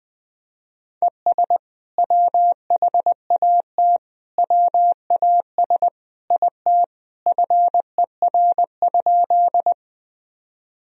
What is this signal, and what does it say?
Morse code, heard as a keyed tone, about 21 wpm, 700 Hz. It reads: ES WHAT WAS IT FER?
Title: Sentences from Top 100 Words and Abbreviations 20wpm